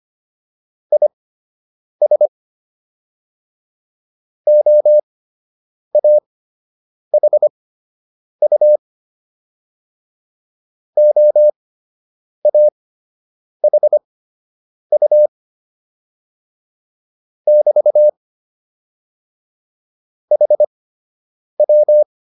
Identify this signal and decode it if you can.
Morse code, heard as a keyed tone, about 25 wpm, spaced as for 8 wpm, 600 Hz. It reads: IS OAHU OAHU <BT> HW